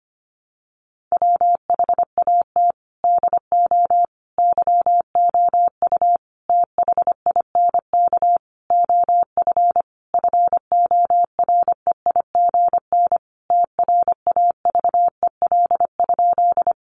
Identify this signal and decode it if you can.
Morse code, heard as a keyed tone, about 25 wpm, 700 Hz. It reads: WHAT DO YOU THINK OF FOREIGN TRAVEL?
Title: Sentences from Top 1400 Words 25wpm